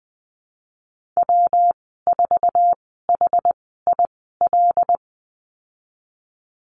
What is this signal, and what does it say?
Morse code, heard as a keyed tone, about 20 wpm, spaced as for 15 wpm, 700 Hz. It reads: W4HIL
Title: 147 Lesson 24g - Letter y - Call Signs 20wpm